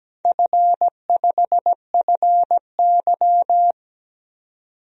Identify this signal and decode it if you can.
Morse code, heard as a keyed tone, about 17 wpm, 700 Hz. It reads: F5FY
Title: DXCC-100 Call Signs - 4 Characters - 17wpm